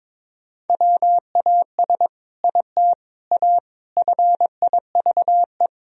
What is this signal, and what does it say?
Morse code, heard as a keyed tone, about 22 wpm, 700 Hz. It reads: WAS IT A FIVE
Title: Sentences from Top 1000 Words - Review Part 2 - 22wpm